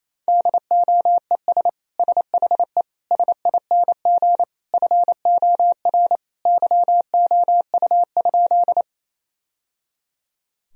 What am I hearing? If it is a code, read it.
Morse code, heard as a keyed tone, about 28 wpm, 690 Hz. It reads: DOES SHE SING FOR YOU?
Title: Sentences from Top 1000 Words - Review Part 2 - 28wpm